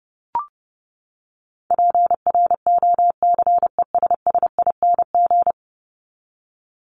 Morse code, heard as a keyed tone, about 30 wpm, 700 Hz. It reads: PROCESSING
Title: ING Suffix - Top 100 Words with Base Words 30wpm